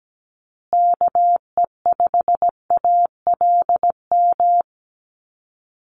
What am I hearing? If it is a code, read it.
Morse code, heard as a keyed tone, about 17 wpm, 700 Hz. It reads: KE5ALM
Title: USA Call Signs Repeated 3x 17wpm